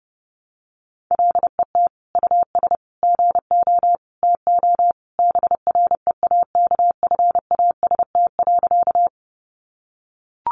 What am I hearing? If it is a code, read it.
Morse code, heard as a keyed tone, about 28 wpm, 700 Hz. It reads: LET US GO TO BREAKFAST.
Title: Sentences from Top 1500 Words 30wpm